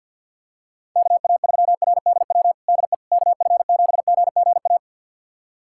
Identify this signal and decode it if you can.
Morse code, heard as a keyed tone, about 50 wpm, 700 Hz. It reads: KA3RDW DE KF7GGA
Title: USA Callsign DE Callsign 50wpm